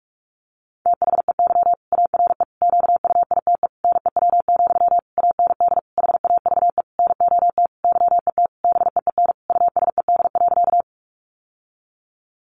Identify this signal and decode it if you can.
Morse code, heard as a keyed tone, about 44 wpm, 700 Hz. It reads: THEY ARE QUITE NEW, AND HAVE NOT YET BEEN USED.
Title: Sentences from Top 700 Words 45wpm